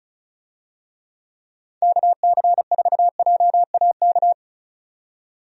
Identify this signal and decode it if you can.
Morse code, heard as a keyed tone, about 36 wpm, 700 Hz. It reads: KC4JAK
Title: USA Call Signs - 6 Characters - 35wpm